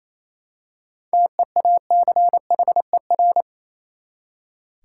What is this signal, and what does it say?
Morse code, heard as a keyed tone, about 28 wpm, 700 Hz. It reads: TEACHER
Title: Seven Letter Words - 28wpm